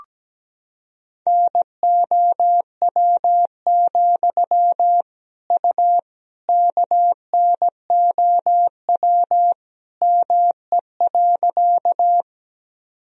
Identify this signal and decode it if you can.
Morse code, heard as a keyed tone, about 17 wpm, 700 Hz. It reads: NOW, U KNOW ME.
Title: Sentences from Top 200 Words and Abbreviations 17wpm